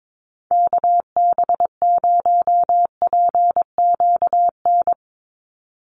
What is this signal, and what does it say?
Morse code, heard as a keyed tone, about 22 wpm, 700 Hz. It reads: KB0PQN
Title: USA Call Signs 22wpm